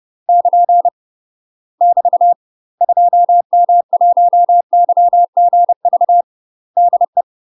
Morse code, heard as a keyed tone, about 30 wpm, 700 Hz. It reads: <KN> <BT> 2M1YGV DE
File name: UK MASTER QSOs TRAINING / 30WPM / UK MASTER QSOs 0033